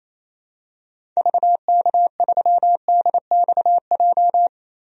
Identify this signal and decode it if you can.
Morse code, heard as a keyed tone, about 28 wpm, 700 Hz. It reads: VK3DXJ
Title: DXCC-100 Call Signs - 6 Characters - 28wpm